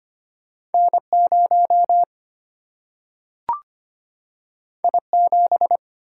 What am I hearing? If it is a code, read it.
Morse code, heard as a keyed tone, about 25 wpm, 700 Hz. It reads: N0 I7
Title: Single-Letter Single-Number 25wpm